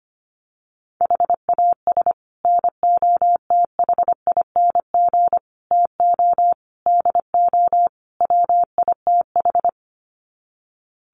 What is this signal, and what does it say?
Morse code, heard as a keyed tone, about 25 wpm, 700 Hz. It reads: HAS NOTHING TO DO WITH
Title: Sets of 5 Words 25wpm